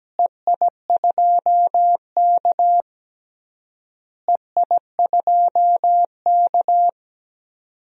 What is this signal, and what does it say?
Morse code, heard as a keyed tone, about 17 wpm, 700 Hz. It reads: EI2K EI2K
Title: DXCC-100 Call Signs - 4 Characters - Repeated 3x - 17wpm